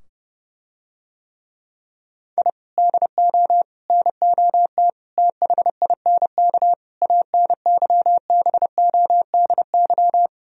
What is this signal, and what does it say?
Morse code, heard as a keyed tone, about 31 wpm, 700 Hz. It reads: I DO NOT THINK ANYBODY